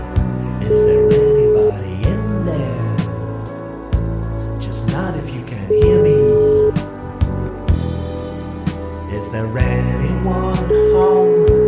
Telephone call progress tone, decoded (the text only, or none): ringback tone